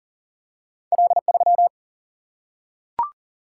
Morse code, heard as a keyed tone, about 42 wpm, 700 Hz. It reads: L3
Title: Single-Letter Single-Number 40wpm